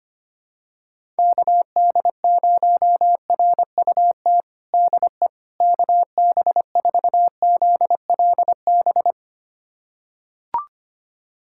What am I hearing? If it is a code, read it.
Morse code, heard as a keyed tone, about 25 wpm, 700 Hz. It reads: KD0RUT DE KB4ZLB
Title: USA Callsign DE Callsign - 6 Characters - 25wpm